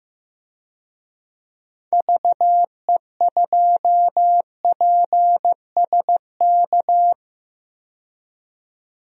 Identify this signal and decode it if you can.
Morse code, heard as a keyed tone, about 15 wpm, 700 Hz. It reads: VE2PSK
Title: DXCC-100 Call Signs - 6 Characters - 15wpm